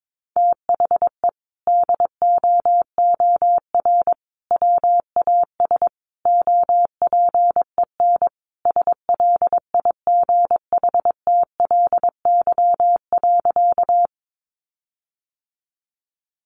Morse code, heard as a keyed tone, about 22 wpm, 700 Hz. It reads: THE DOOR WAS OPEN SLIGHTLY.
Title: Sentences from Top 1300 Words 22wpm